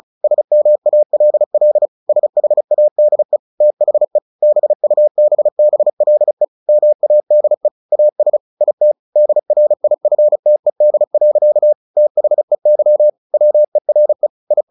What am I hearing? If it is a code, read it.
Morse code, heard as a keyed tone, about 35 wpm, 600 Hz. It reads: SMALL SHADE THE BUBBLE MADE AS IT DRIFTED. THEY WERE I
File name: the old man and the sea / the old man and the sea 1 35wpm